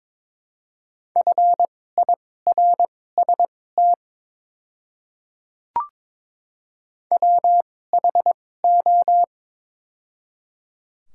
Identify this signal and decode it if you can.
Morse code, heard as a keyed tone, about 23 wpm, spaced as for 16 wpm, 700 Hz. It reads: FIRST WHO
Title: 2x Character Spacing - Top 100 Words - 22wpm